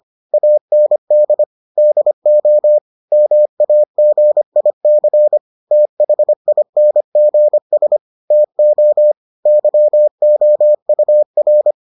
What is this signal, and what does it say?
Morse code, heard as a keyed tone, about 25 wpm, 600 Hz. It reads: AND DO MAGIC THINGS TO YOUR